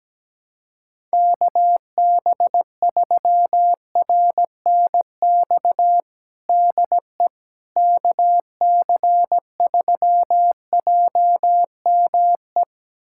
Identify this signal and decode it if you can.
Morse code, heard as a keyed tone, about 17 wpm, 700 Hz. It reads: KB3RNX DE KC3JME
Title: USA Callsign DE Callsign 17wpm